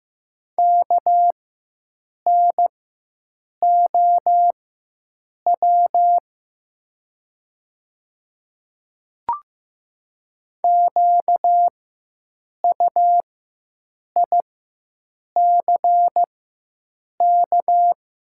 Morse code, heard as a keyed tone, about 15 wpm, spaced as for 7 wpm, 700 Hz. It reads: KNOW QUICK
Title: 4x Character Spacing - Top 500 Words - 15wpm